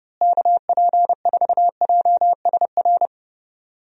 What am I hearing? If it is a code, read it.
Morse code, heard as a keyed tone, about 30 wpm, 700 Hz. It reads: KP4JSR
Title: USA Call Signs - 6 Characters - Repeated 2x - 30wpm